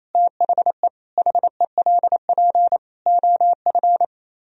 Morse code, heard as a keyed tone, about 38 wpm, 700 Hz. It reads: THE HELP OF